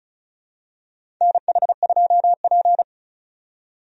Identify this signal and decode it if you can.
Morse code, heard as a keyed tone, about 34 wpm, 700 Hz. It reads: NH2P
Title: USA Call Signs - 4 Characters - 35wpm